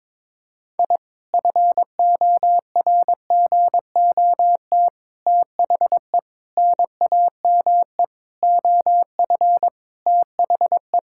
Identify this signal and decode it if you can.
Morse code, heard as a keyed tone, about 22 wpm, 700 Hz. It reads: I FORGOT THE NAME OF THE
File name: practice / 22wpm / Sentences with 4-Word Phrases - 22wpm